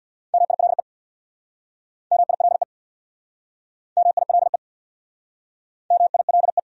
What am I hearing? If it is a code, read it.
Morse code, heard as a keyed tone, about 66 wpm, 700 Hz. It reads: MILE MILE MILE MILE